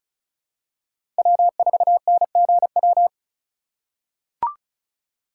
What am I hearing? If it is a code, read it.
Morse code, heard as a keyed tone, about 36 wpm, 700 Hz. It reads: W4NGW
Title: USA Call Signs - 5 Characters - 2x Character Spacing - 35wpm